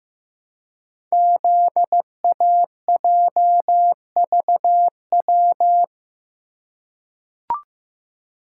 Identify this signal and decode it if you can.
Morse code, heard as a keyed tone, about 15 wpm, 700 Hz. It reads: ZAJVW
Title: Code Groups - 5 Letters - 15wpm